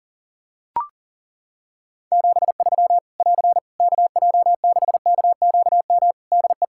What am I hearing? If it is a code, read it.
Morse code, heard as a keyed tone, about 42 wpm, 700 Hz. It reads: 73 <AR> KJ6KQM DE
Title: Practice QSO - Exchange 4 - Ending - 40wpm